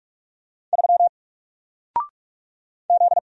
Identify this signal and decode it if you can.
Morse code, heard as a keyed tone, about 45 wpm, 700 Hz. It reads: 3 Z
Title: Single Letter-Number 45wpm